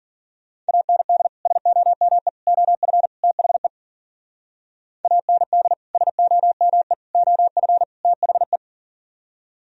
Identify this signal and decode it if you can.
Morse code, heard as a keyed tone, about 51 wpm, 700 Hz. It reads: AND SOME OF THE AND SOME OF THE